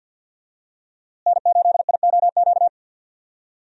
Morse code, heard as a keyed tone, about 47 wpm, 700 Hz. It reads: N8IOQ
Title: USA Call Signs - 5 Characters - Repeated 2x - 50wpm